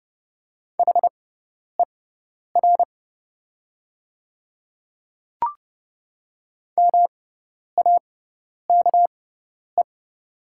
Morse code, heard as a keyed tone, about 30 wpm, spaced as for 10 wpm, 700 Hz. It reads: HER MAKE
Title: 6x Character Spacing - Top 100 Words - 30wpm